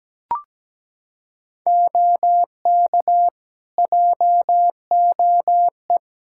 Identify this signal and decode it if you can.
Morse code, heard as a keyed tone, about 17 wpm, 700 Hz. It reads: OK JOE
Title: Practice QSO - Exchange 3 - More Info - 17wpm